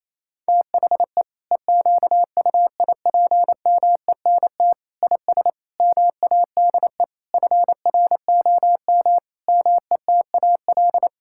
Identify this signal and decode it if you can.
Morse code, heard as a keyed tone, about 27 wpm, 700 Hz. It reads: THE EQUIPMENT IS MADE FROM METAL
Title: Sentences from Top 1000 Words - Review Part 6 - 28wpm